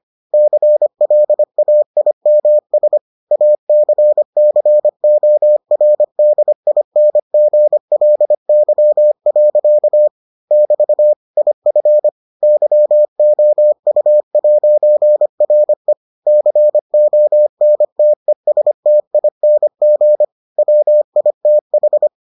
Morse code, heard as a keyed tone, about 25 wpm, 600 Hz. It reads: CLAIMS ACCORDINGLY. <BT> IF YOU'RE CONTESTING WITH